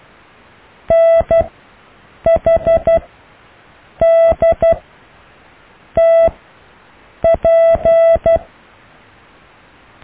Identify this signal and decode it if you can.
Morse code, heard as a keyed tone, about 12 wpm, spaced as for 6 wpm, 660 Hz. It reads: NHDTP
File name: cw / EKTPV+BFHNW